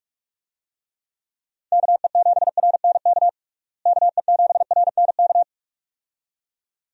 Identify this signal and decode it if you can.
Morse code, heard as a keyed tone, about 46 wpm, 700 Hz. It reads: KE7RNK KE7RNK
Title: USA Call Signs - 6 Characters - Repeated 2x - 45wpm